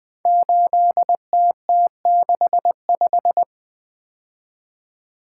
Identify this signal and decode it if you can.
Morse code, heard as a keyed tone, about 20 wpm, 700 Hz. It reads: 8TT65
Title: Code Groups - 5 Letter:Numbers - 20wpm